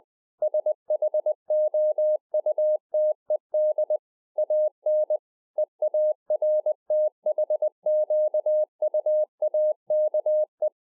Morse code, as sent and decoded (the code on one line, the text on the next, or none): ... .... --- ..- - . -.. / .- -. / . .- .-. - .... --.- ..- .- -.- .
SHOUTED AN EARTHQUAKE